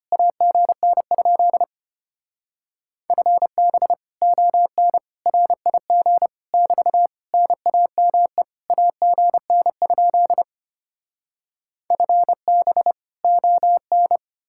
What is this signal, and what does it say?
Morse code, heard as a keyed tone, about 33 wpm, 700 Hz. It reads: AGN? FB ON RIG <BT> NAME AGN? FB ON